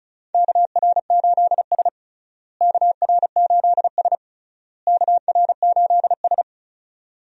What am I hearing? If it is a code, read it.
Morse code, heard as a keyed tone, about 36 wpm, 700 Hz. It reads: KR8S KR8S KR8S
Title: USA Call Signs - 4 Characters - Repeated 3x - 35wpm